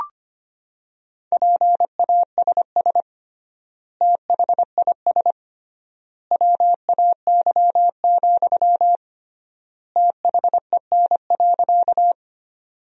Morse code, heard as a keyed tone, about 25 wpm, 700 Hz. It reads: PASS THIS WAY, THEN.
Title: 3x Word-Spacing - Sentences from Top 400 Words - 25wpm